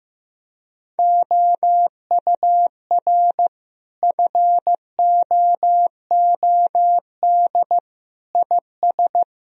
Morse code, heard as a keyed tone, about 15 wpm, 700 Hz. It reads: OUR FOOD IS
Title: Sentences from Top 1000 Words - Review Part 1 - 15wpm